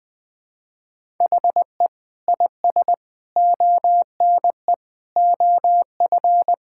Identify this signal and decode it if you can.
Morse code, heard as a keyed tone, about 20 wpm, 700 Hz. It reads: HE IS ONE OF
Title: Sentences with 3-Word Phrases - 20wpm